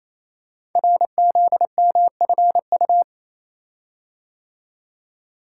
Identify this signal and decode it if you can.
Morse code, heard as a keyed tone, about 27 wpm, 700 Hz. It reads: RZMFU
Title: Code Groups - 5 Letter:Numbers - 28wpm